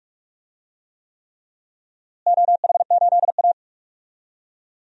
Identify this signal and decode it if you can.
Morse code, heard as a keyed tone, about 42 wpm, 700 Hz. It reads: OH8A
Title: DXCC-100 Call Signs - 4 Characters - Repeated 3x - 45wpm